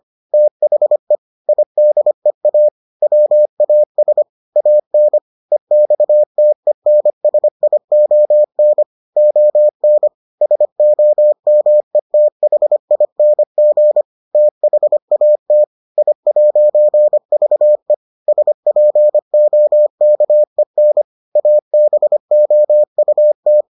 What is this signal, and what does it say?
Morse code, heard as a keyed tone, about 25 wpm, 600 Hz. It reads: THE IDEA WAS AN EXTENSION ON SOMETHING THAT I'VE SPOKEN ABOUT